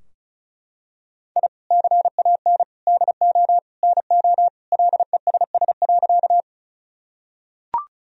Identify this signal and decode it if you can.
Morse code, heard as a keyed tone, about 36 wpm, 700 Hz. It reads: I CAN DO NO LESS.
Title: Sentences from Top 1000 Words - Review Part 2 - 35wpm